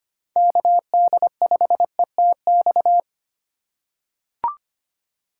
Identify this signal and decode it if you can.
Morse code, heard as a keyed tone, about 25 wpm, 700 Hz. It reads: KD5ETX
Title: USA Call Signs - 6 Characters - 2x Character Spacing - 25wpm